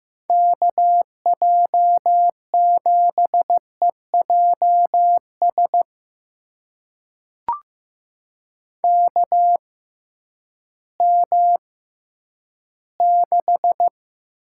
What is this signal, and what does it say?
Morse code, heard as a keyed tone, about 15 wpm, spaced as for 10 wpm, 700 Hz. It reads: KJ7EJS KM6
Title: USA Call Signs - 6 Characters - 6x Character Spacing - 15wpm